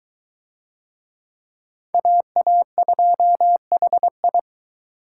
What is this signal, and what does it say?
Morse code, heard as a keyed tone, about 22 wpm, 700 Hz. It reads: AA2HI